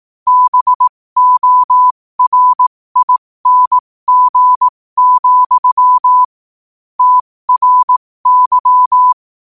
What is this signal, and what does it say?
Morse code, heard as a keyed tone, about 18 wpm, spaced as for 15 wpm, 1000 Hz. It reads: BORING, TRY
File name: qotdfiles / 15wpm